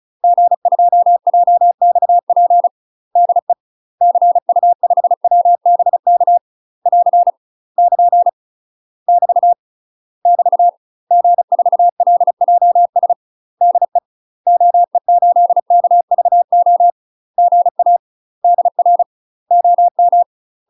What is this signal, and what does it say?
Morse code, heard as a keyed tone, about 35 wpm, 700 Hz. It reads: G2JXP DE CU5WBK <AR> <KN> <BT> <BT> G4LJS DE OE8KVO GA DR OM
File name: UK MASTER QSOs TRAINING / 35WPM / UK MASTER QSOs 0022